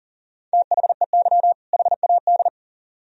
Morse code, heard as a keyed tone, about 53 wpm, 700 Hz. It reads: THEY HAD